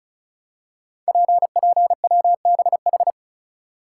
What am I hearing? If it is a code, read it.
Morse code, heard as a keyed tone, about 36 wpm, 700 Hz. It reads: PPWBH
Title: Code Groups - 5 Letters - 35wpm